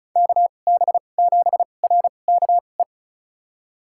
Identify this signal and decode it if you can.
Morse code, heard as a keyed tone, about 36 wpm, spaced as for 25 wpm, 700 Hz. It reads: KB7RKE
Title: USA Call Signs - 6 Characters - 2x Character Spacing - 35wpm